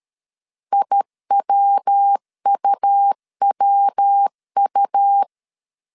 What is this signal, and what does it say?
Morse code, heard as a keyed tone, about 13 wpm, 800 Hz. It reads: IWUWU
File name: CW / DET / 1 3 1500C S B 12wpm 2828